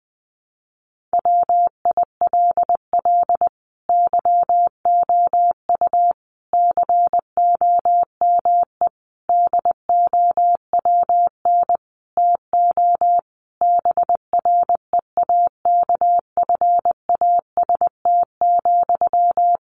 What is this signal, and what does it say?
Morse code, heard as a keyed tone, about 20 wpm, 700 Hz. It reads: WILL YOU COME DOWN TO BREAKFAST,
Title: Sentences from Top 1500 Words 20wpm